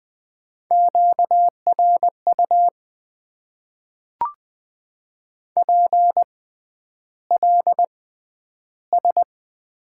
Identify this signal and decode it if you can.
Morse code, heard as a keyed tone, about 20 wpm, spaced as for 10 wpm, 700 Hz. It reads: QRU PLS
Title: Top 90 QSO Elements - 6x Character Spacing - 20wpm